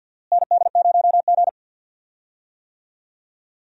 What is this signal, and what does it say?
Morse code, heard as a keyed tone, about 52 wpm, 700 Hz. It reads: ND0G
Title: USA Call Signs - 4 Characters - Repeated 2x - 50wpm